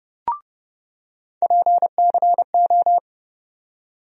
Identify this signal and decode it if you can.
Morse code, heard as a keyed tone, about 30 wpm, 700 Hz. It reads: PCO
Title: USA Call Sign Postfixes - 3 Letters 30wpm